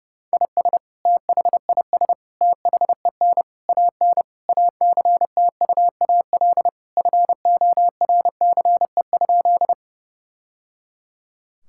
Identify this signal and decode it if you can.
Morse code, heard as a keyed tone, about 30 wpm, 700 Hz. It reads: IS THIS THEN AN ACTUAL FORCE?
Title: Sentences from Top 1000 Words - Review Part 5 - 30wpm